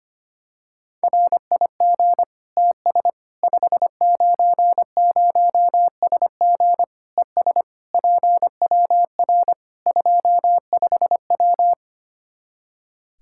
Morse code, heard as a keyed tone, about 25 wpm, 700 Hz. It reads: RIG TS 590SG ES PWR 25W
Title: Practice QSO - Exchange 2 - Info - 25wpm